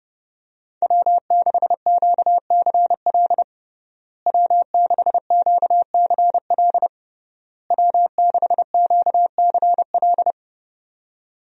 Morse code, heard as a keyed tone, about 28 wpm, 700 Hz. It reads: W6QCL W6QCL W6QCL
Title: USA Call Signs - 5 Characters - Repeated 3x - 30wpm